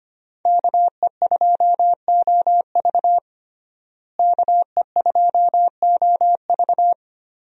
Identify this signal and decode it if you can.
Morse code, heard as a keyed tone, about 25 wpm, 700 Hz. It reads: KE2OV KE2OV
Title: USA Call Signs - 5 Characters - Repeated 3x - 25wpm